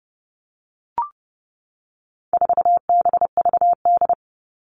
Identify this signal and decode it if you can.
Morse code, heard as a keyed tone, about 30 wpm, 700 Hz. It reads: 4BVD
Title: All Cut Numbers - 4-Digit 30wpm